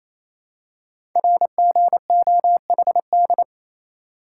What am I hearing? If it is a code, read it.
Morse code, heard as a keyed tone, about 28 wpm, 700 Hz. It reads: RGOHD
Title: Code Groups - 5 Letters - 28wpm